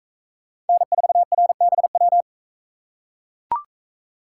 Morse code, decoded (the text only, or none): N4RBW